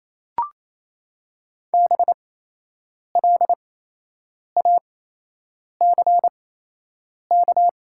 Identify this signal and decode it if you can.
Morse code, heard as a keyed tone, about 28 wpm, spaced as for 8 wpm, 700 Hz. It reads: BLACK
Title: 8x Character Spacing - Top 500 Words - 28wpm